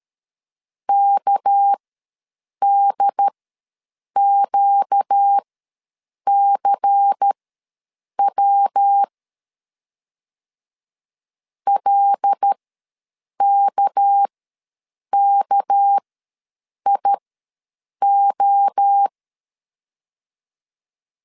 Morse code, heard as a keyed tone, about 13 wpm, spaced as for 7 wpm, 800 Hz. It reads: KDQCW LKKIO
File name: LEZIONI / 6 PKX0 / 6 PK ws 12wpm 3116